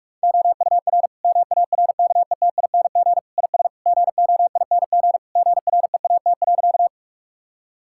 Morse code, heard as a keyed tone, about 41 wpm, 700 Hz. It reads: OUR MARKETING IS GOING GREAT.
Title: Sentences from Top 1000 Words - Review Part 5 - 45wpm